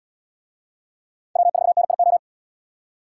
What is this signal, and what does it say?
Morse code, heard as a keyed tone, about 77 wpm, 700 Hz. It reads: W4NIO